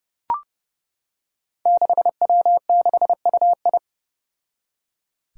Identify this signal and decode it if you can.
Morse code, heard as a keyed tone, about 30 wpm, 700 Hz. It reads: 6W6UI
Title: Five Challenging Letter-Number Combinations 30wpm